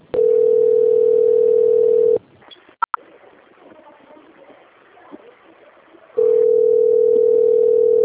Ringback tone (North America), 440 + 480 Hz.